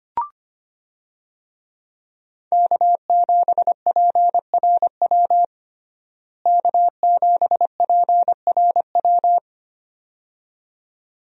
Morse code, heard as a keyed tone, about 25 wpm, 700 Hz. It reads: K7PRW K7PRW
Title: USA Call Signs - 5 Characters - Repeated 2x - 25wpm